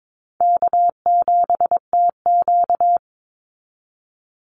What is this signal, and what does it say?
Morse code, heard as a keyed tone, about 22 wpm, 700 Hz. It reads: K7TQ
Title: NA SOTA Chaser Callsigns 22wpm